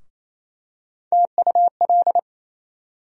Morse code, heard as a keyed tone, about 28 wpm, 700 Hz. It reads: TUL
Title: USA Call Sign Postfixes - 3 Letters 28wpm